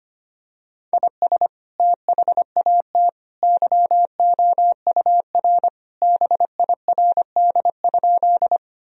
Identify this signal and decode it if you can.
Morse code, heard as a keyed tone, about 25 wpm, 700 Hz. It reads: IS THAT YOUR BIRD?
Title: Sentences from Top 1000 Words - Review Part 2 - 25wpm